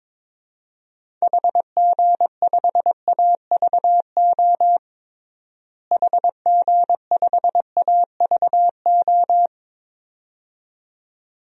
Morse code, heard as a keyed tone, about 22 wpm, 700 Hz. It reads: HG5AVO HG5AVO
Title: DXCC-100 Call Signs - 6 Characters - Repeated 2x - 22wpm